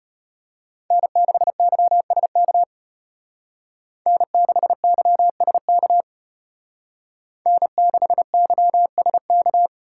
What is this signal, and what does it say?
Morse code, heard as a keyed tone, about 33 wpm, 700 Hz. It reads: N6YSK N6YSK N6YSK